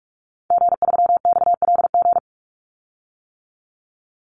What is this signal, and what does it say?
Morse code, heard as a keyed tone, about 44 wpm, 700 Hz. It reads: G3XLG
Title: Call Signs - 45wpm